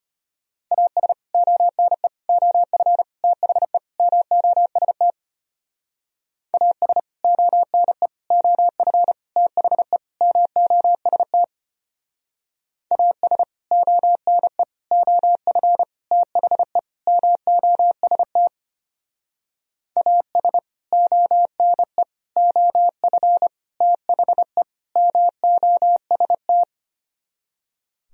Morse code, heard as a keyed tone, about 33 wpm, 700 Hz. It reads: AS ONE OF THE MOST AS ONE OF THE MOST AS ONE OF THE MOST AS ONE OF THE MOST